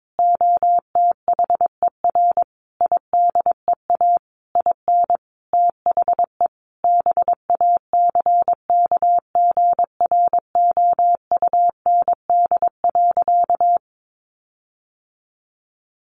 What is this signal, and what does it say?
Morse code, heard as a keyed tone, about 22 wpm, 700 Hz. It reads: OTHER IDEA IN THE BACKGROUND.
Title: Sentences from Top 1300 Words 22wpm